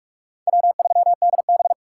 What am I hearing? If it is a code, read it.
Morse code, heard as a keyed tone, about 46 wpm, 700 Hz. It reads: W3DB
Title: USA Call Signs - 4 Characters - 2x Character Spacing - 45wpm